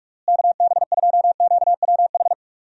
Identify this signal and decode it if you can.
Morse code, heard as a keyed tone, about 48 wpm, 700 Hz. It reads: KB2QWH
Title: USA Call Signs - 6 Characters - Repeated 3x - 45wpm